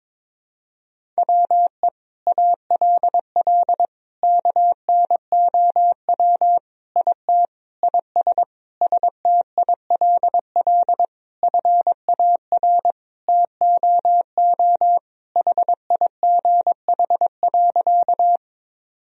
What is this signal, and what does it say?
Morse code, heard as a keyed tone, about 22 wpm, 700 Hz. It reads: WE ALL KNOW IT IS STILL FAR TOO HIGH.
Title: Sentences from Top 1000 Words - Review Part 1 - 22wpm